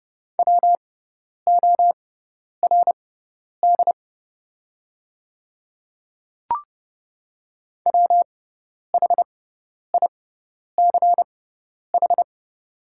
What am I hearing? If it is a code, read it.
Morse code, heard as a keyed tone, about 30 wpm, spaced as for 10 wpm, 690 Hz. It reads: WORD WHICH
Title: 6x Character Spacing - Top 200 Words - 30wpm